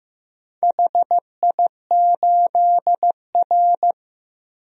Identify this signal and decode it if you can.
Morse code, heard as a keyed tone, about 15 wpm, 700 Hz. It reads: HI8R
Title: DXCC-100 Call Signs - 4 Characters - Repeated 2x - 15wpm